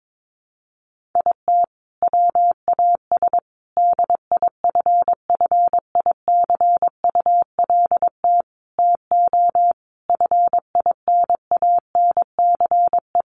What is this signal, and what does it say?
Morse code, heard as a keyed tone, about 22 wpm, 700 Hz. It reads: IT WAS DIFFICULT TO FINANCE
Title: Sentences from Top 1500 Words 22wpm